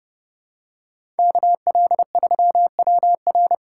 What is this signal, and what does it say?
Morse code, heard as a keyed tone, about 30 wpm, 700 Hz. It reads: KL3WR
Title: USA Call Signs - 5 Characters - 3x Character Spacing - 30wpm